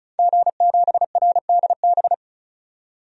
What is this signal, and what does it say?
Morse code, heard as a keyed tone, about 36 wpm, 700 Hz. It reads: G7RDB
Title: DXCC-100 Call Signs - 5 Characters - 35wpm